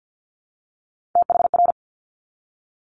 Morse code, heard as a keyed tone, about 50 wpm, 690 Hz. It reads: THR
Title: Multi-letter Phonemes 50wpm